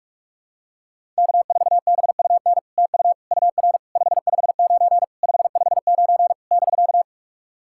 Morse code, heard as a keyed tone, about 44 wpm, 700 Hz. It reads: K4BUN TU UR 559 559 <BK>
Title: SOTA QSO Practice 45wpm